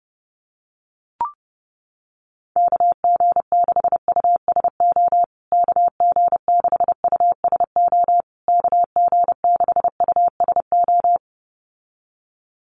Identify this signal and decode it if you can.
Morse code, heard as a keyed tone, about 30 wpm, 700 Hz. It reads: KG6USO KG6USO KG6USO
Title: USA Call Signs Repeated 3x 30wpm